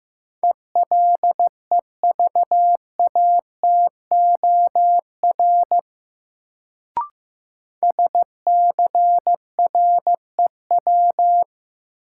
Morse code, heard as a keyed tone, about 16 wpm, 700 Hz. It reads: ELEVATOR SCREW
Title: Sentences from Top 1600 Words - New Word Drill Alt - 15wpm